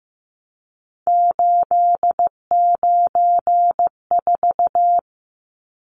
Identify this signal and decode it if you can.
Morse code, heard as a keyed tone, about 15 wpm, 700 Hz. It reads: 894
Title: Numbers - 3-Digit 15wpm